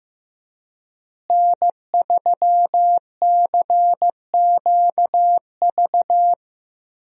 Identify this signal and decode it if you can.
Morse code, heard as a keyed tone, about 15 wpm, 700 Hz. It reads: N3CQV
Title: USA Call Signs - 5 Characters - Repeated 2x - 15wpm